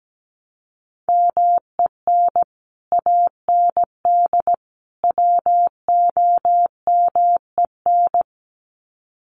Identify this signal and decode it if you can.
Morse code, heard as a keyed tone, about 17 wpm, 700 Hz. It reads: MEN AND WOMEN
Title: Sets of 3 Words 17wpm